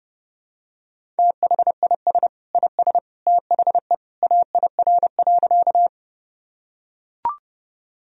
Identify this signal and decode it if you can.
Morse code, heard as a keyed tone, about 32 wpm, 700 Hz. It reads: THIS IS THE AIR.
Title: Sentences from Top 1000 Words - Review Part 1 - 30wpm